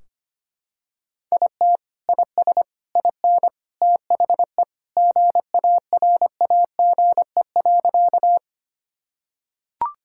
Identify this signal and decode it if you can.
Morse code, heard as a keyed tone, about 25 wpm, 700 Hz. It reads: IT IS IN THE GARAGE.
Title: Sentences from Top 1600 Words 25wpm